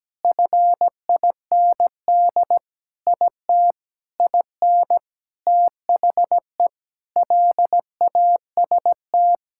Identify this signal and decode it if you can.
Morse code, heard as a keyed tone, about 17 wpm, 700 Hz. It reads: FIND IT IN THE LAST
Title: Sentences from Top 1000 Words - Review Part 5 - 17wpm